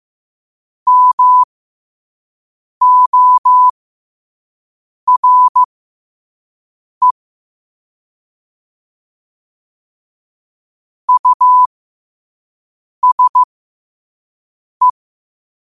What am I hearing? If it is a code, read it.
Morse code, heard as a keyed tone, about 15 wpm, spaced as for 5 wpm, 1000 Hz. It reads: MORE USE
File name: qotdfiles / 5wpm